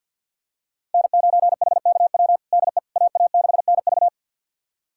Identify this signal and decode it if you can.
Morse code, heard as a keyed tone, about 46 wpm, 700 Hz. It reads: N9HKW DE AA6NV
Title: USA Callsign DE Callsign - 5 Characters - 50wpm